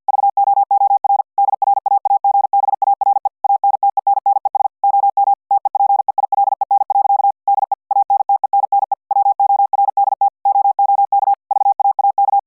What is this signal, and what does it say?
Morse code, heard as a keyed tone, about 50 wpm, 800 Hz. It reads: VOOR DRAAGBARE ANTENNES OM TE PEILEN. DE ANTENNE WORDT OOK VAAK